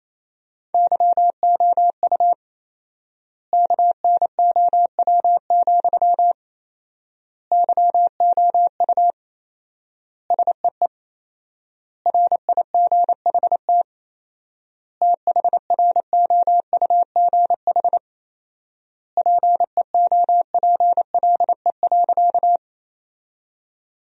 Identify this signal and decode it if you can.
Morse code, heard as a keyed tone, about 28 wpm, 700 Hz. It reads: YOU KNOW, YOU SEE RIGHT THROUGH PEOPLE.
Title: 4x Word-Spacing - Sentences from Top 200 Words - 28wpm